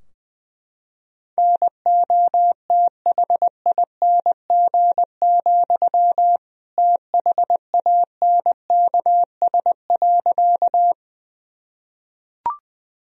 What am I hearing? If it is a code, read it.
Morse code, heard as a keyed tone, about 20 wpm, 700 Hz. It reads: NOTHING, THANKS.
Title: Sentences from Top 1000 Words - Review Part 3 - 20wpm